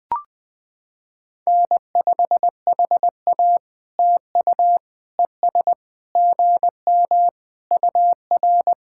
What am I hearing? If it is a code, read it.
Morse code, heard as a keyed tone, about 20 wpm, 700 Hz. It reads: N5HA TU ES GM UR
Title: POTA QSO Practice - 20wpm